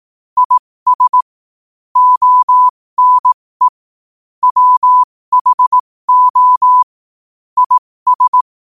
Morse code, heard as a keyed tone, about 18 wpm, spaced as for 15 wpm, 1000 Hz. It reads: IS ONE WHO IS
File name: qotdfiles / 15wpm